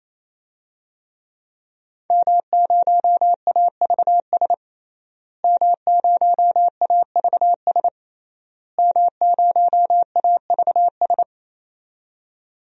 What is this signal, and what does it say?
Morse code, heard as a keyed tone, about 28 wpm, 700 Hz. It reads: M0AVS M0AVS M0AVS
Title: DXCC-100 Call Signs - 5 Characters - Repeated 3x - 28wpm